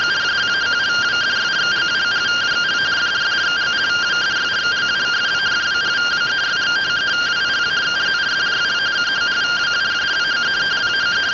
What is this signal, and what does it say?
Two-tone teleprinter signal, about 170 Hz shift.